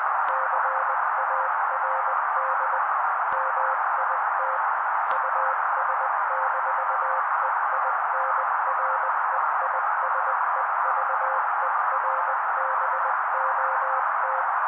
Morse code, heard as a keyed tone, about 20 wpm, 530 Hz. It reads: CARD MIT US-EINREISEVERBOT
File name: daily / 18wpm noise